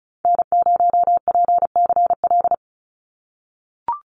Morse code, heard as a keyed tone, about 34 wpm, 700 Hz. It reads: N0PCL
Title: NA SOTA Activator Callsigns 35wpm